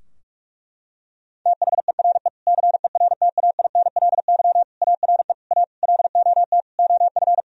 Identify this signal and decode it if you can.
Morse code, heard as a keyed tone, about 46 wpm, 700 Hz. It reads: THERE CERTAINLY ARE A LOT OF
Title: Sentences from Top 1000 Words - Review Part 3 - 45wpm